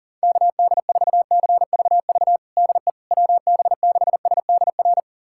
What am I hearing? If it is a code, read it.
Morse code, heard as a keyed tone, about 40 wpm, 700 Hz. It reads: KD4CVV DE WB6SDR
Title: USA Callsign DE Callsign - 6 Characters - 40wpm